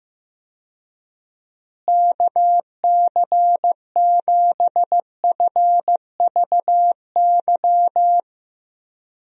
Morse code, heard as a keyed tone, about 15 wpm, 700 Hz. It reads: KC7FVY
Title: USA Call Signs - 6 Characters - Repeated 2x - 15wpm